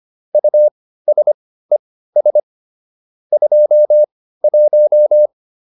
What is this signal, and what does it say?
Morse code, heard as a keyed tone, about 25 wpm, spaced as for 15 wpm, 600 Hz. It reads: USES 21